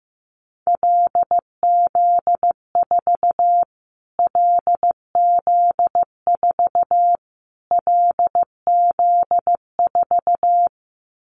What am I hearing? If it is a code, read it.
Morse code, heard as a keyed tone, about 15 wpm, 700 Hz. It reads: LZ4 LZ4 LZ4
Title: Callsign Prefixes for Top 25 DXCC - Repeated 3x - 15wpm